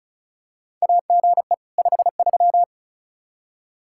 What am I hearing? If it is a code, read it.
Morse code, heard as a keyed tone, about 36 wpm, 700 Hz. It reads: AGE 53
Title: K1USN SST QSO Practice 35wpm